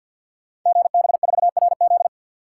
Morse code, heard as a keyed tone, about 52 wpm, 700 Hz. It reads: GB4RZ
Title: Code Groups - 5 Letter:Numbers - 50wpm